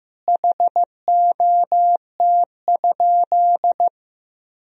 Morse code, heard as a keyed tone, about 15 wpm, 700 Hz. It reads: HOT?
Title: Sentences from Top 1000 Words - Review Part 3 - 15wpm